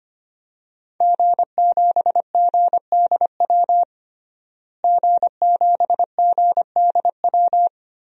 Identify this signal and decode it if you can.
Morse code, heard as a keyed tone, about 25 wpm, 700 Hz. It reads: G7GDW G7GDW
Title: DXCC-100 Call Signs - 5 Characters - Repeated 2x - 25wpm